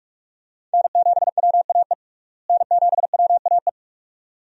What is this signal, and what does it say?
Morse code, heard as a keyed tone, about 44 wpm, 700 Hz. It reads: N7WAE N7WAE
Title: USA Call Signs - 5 Characters - Repeated 2x - 45wpm